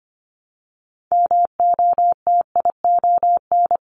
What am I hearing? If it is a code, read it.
Morse code, heard as a keyed tone, about 25 wpm, 700 Hz. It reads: MOTION
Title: Multi-letter Phonemes Repeated 3x 25wpm